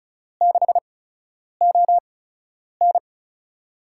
Morse code, heard as a keyed tone, about 36 wpm, spaced as for 10 wpm, 700 Hz. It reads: 6ON
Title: USA Call Signs - 8x Character Spacing - 35wpm